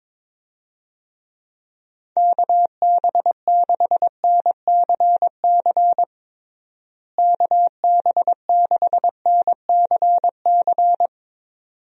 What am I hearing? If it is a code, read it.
Morse code, heard as a keyed tone, about 22 wpm, 700 Hz. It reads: KB6NCC KB6NCC
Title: USA Call Signs - 6 Characters - Repeated 2x - 22wpm